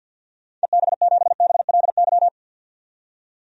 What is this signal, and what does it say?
Morse code, heard as a keyed tone, about 46 wpm, 700 Hz. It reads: EB7BLY
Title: DXCC-100 Call Signs - 6 Characters - 50wpm